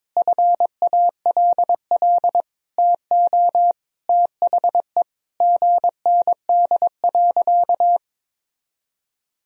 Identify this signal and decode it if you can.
Morse code, heard as a keyed tone, about 22 wpm, 700 Hz. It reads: FALL TO THE GND.
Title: Sentences from Top 500 Words and Abbreviations 22wpm